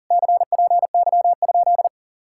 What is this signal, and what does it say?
Morse code, heard as a keyed tone, about 48 wpm, 700 Hz. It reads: CPY?
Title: Speed-Racing Common QSO Phrases Encore - 40wpm